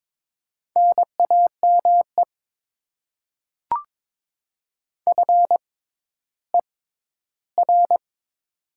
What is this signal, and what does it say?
Morse code, heard as a keyed tone, about 22 wpm, spaced as for 12 wpm, 700 Hz. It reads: NAME FER
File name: Top 20 QSO Elements - 6x Character Spacing - 22wpm